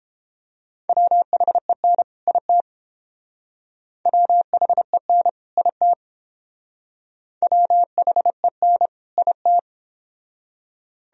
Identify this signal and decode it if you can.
Morse code, heard as a keyed tone, about 30 wpm, 700 Hz. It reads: WHEN IT WHEN IT WHEN IT